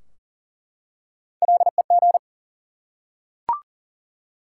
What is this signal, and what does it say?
Morse code, heard as a keyed tone, about 40 wpm, 700 Hz. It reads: LEG
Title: Three Letter Words - 40wpm